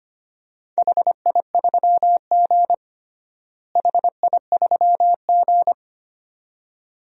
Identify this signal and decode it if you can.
Morse code, heard as a keyed tone, about 25 wpm, 700 Hz. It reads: HI3G HI3G
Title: DXCC-100 Call Signs - 4 Characters - Repeated 2x - 25wpm